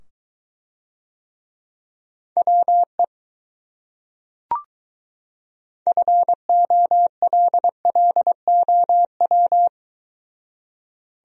Morse code, heard as a keyed tone, about 22 wpm, 700 Hz. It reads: WE FOLLOW